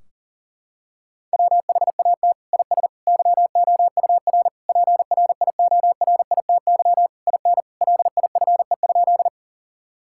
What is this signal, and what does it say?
Morse code, heard as a keyed tone, about 37 wpm, 700 Hz. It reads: WHAT IS YOUR PRIORITY IN LIFE?
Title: Sentences from Top 1600 Words 40wpm